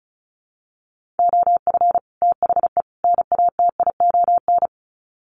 Morse code, heard as a keyed tone, about 34 wpm, 700 Hz. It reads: OF THE NATION
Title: Sets of 3 Words 35wpm